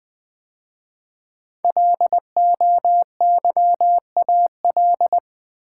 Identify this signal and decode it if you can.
Morse code, heard as a keyed tone, about 20 wpm, 700 Hz. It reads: LOYAL
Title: 5 Letter Words - First 3 Letters Given - 20wpm